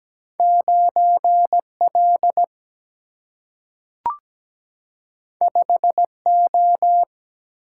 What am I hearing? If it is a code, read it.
Morse code, heard as a keyed tone, about 17 wpm, 700 Hz. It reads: 9L 5O
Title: Single-Letter Single-Number 17wpm